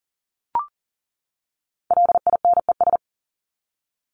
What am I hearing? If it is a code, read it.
Morse code, heard as a keyed tone, about 40 wpm, 700 Hz. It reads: LINES
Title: ES:S Suffix - Top 100 Words with Base Words 40wpm